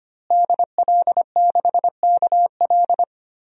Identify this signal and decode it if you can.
Morse code, heard as a keyed tone, about 25 wpm, 700 Hz. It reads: DL6KL
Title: DXCC-100 Call Signs - 5 Characters - Repeated 3x - 25wpm